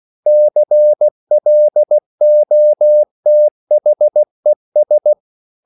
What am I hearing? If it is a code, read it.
Morse code, heard as a keyed tone, about 16 wpm, 600 Hz. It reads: CLOTHES